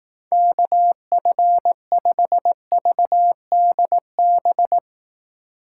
Morse code, heard as a keyed tone, about 18 wpm, 700 Hz. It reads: KF5VDB
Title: Speed Racing - USA Call Signs 15wpm